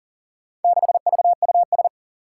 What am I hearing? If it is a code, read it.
Morse code, heard as a keyed tone, about 40 wpm, 700 Hz. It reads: 6VUS
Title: Four Challenging Letter-Number Combinations 40wpm